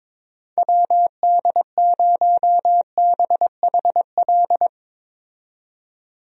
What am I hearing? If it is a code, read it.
Morse code, heard as a keyed tone, about 21 wpm, 700 Hz. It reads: WD0BHL
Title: USA Call Signs - 6 Characters - 22wpm